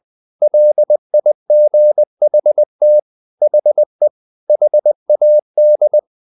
Morse code, heard as a keyed tone, about 20 wpm, 600 Hz. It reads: LIGHT HE HAD
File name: the old man and the sea / the old man and the sea part 1 20wpm